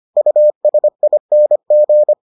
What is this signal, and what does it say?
Morse code, heard as a keyed tone, about 25 wpm, 600 Hz. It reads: USING